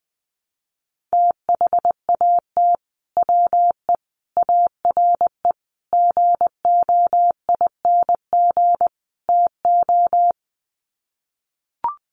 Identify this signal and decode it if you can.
Morse code, heard as a keyed tone, about 20 wpm, 700 Hz. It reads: THAT WE ARE GOING TO
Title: Sets of 5 Words Encore 20wpm